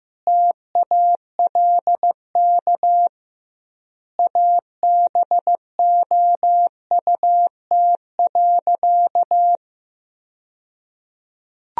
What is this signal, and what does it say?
Morse code, heard as a keyed tone, about 16 wpm, 700 Hz. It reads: TALK ABOUT.
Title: 2x Word-Spacing - Sentences from Top 400 Words - 15wpm